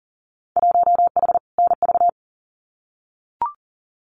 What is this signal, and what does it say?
Morse code, heard as a keyed tone, about 40 wpm, 700 Hz. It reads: 1H NV
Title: Winter Field Day Exchanges 40wpm